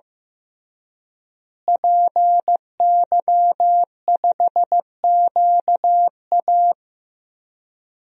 Morse code, heard as a keyed tone, about 16 wpm, 700 Hz. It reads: PY5QA
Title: DXCC-100 Call Signs - 5 Characters - Repeated 3x - 15wpm